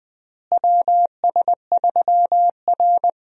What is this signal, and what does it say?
Morse code, heard as a keyed tone, about 20 wpm, 700 Hz. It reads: WS3R
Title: USA Call Signs - 4 Characters - Repeated 3x - 20wpm